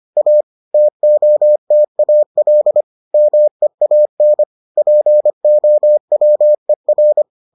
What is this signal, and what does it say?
Morse code, heard as a keyed tone, about 25 wpm, 600 Hz. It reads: A TOTAL MEAN POWER